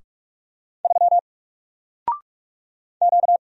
Morse code, heard as a keyed tone, about 46 wpm, 700 Hz. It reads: 3 Q
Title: Single Letter-Number Alt 45wpm